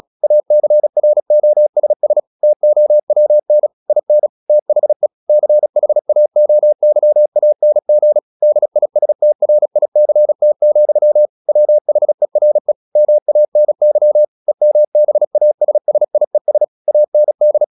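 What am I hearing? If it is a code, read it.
Morse code, heard as a keyed tone, about 36 wpm, 600 Hz. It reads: ACROSS TOWN IN THE CHAOYANG DISTRICT, WHERE MANY EMBASSIES AND
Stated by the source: Morse practice at 35 wpm from CW med Gnister, sending Ken Follett's novel Never in English.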